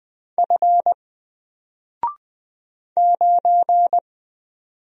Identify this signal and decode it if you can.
Morse code, heard as a keyed tone, about 20 wpm, 700 Hz. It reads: F 9
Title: Single Character (Letter:Number:Punctuation:Prosign) 20wpm